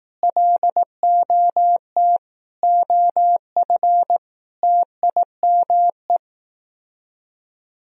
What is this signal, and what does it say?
Morse code, heard as a keyed tone, about 18 wpm, 700 Hz. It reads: LOT OF TIME
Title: Speed-Racing - Sets of 5 Words 15wpm